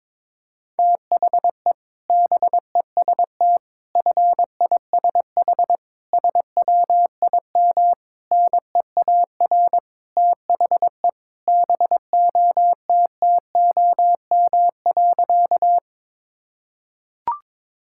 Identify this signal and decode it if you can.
Morse code, heard as a keyed tone, about 22 wpm, 700 Hz. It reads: THE BEST FISH SWIM NEAR THE BOTTOM.
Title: Sentences from Top 1000 Words - Review Part 6 - 22wpm